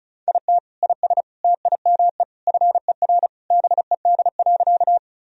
Morse code, heard as a keyed tone, about 36 wpm, 700 Hz. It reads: IT IS TIME FER BED.
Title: Sentences from Top 500 Words and Abbreviations 35wpm